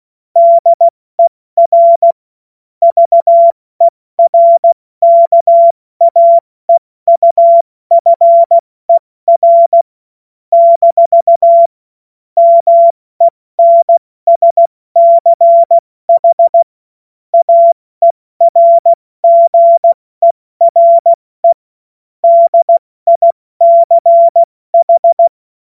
Morse code, heard as a keyed tone, about 16 wpm, 680 Hz. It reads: DER VERKAEUFER - MENSCH AERGERE DICH
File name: Morsekurs / Witze / Witz 20. Dezember 2022 70 F80